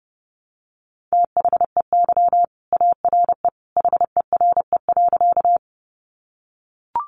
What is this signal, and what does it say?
Morse code, heard as a keyed tone, about 30 wpm, 700 Hz. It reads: THEY ARE HERE.
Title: Sentences from Top 200 Words 30wpm